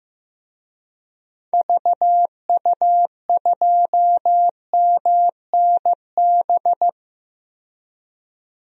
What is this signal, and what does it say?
Morse code, heard as a keyed tone, about 15 wpm, 700 Hz. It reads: VU2MNB
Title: DXCC-100 Call Signs - 6 Characters - 15wpm